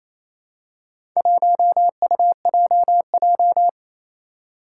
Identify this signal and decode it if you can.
Morse code, heard as a keyed tone, about 28 wpm, 700 Hz. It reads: 1UJJ
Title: Four Challenging Letter-Number Combinations 28wpm